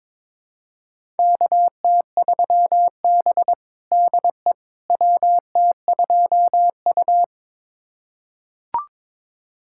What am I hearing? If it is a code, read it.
Morse code, heard as a keyed tone, about 22 wpm, 700 Hz. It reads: KT3B DE WT2U
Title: USA Callsign DE Callsign - 4 Characters - 22wpm